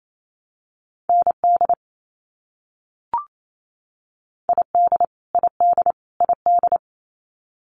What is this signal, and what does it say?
Morse code, heard as a keyed tone, about 28 wpm, 700 Hz. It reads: ND ID ID ID
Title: US States Repeated 3x 28wpm